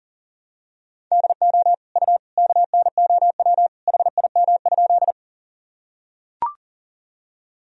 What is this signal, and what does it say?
Morse code, heard as a keyed tone, about 40 wpm, 700 Hz. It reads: DO U KNOW HIM?
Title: Sentences from Top 100 Words and Abbreviations 40wpm